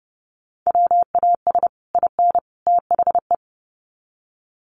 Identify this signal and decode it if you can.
Morse code, heard as a keyed tone, about 30 wpm, 700 Hz. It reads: WAS IN THE
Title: Sets of 3 Words 30wpm